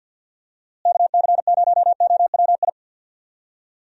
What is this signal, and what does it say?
Morse code, heard as a keyed tone, about 46 wpm, 700 Hz. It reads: KC0OWI
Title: USA Call Signs - 6 Characters - Repeated 3x - 50wpm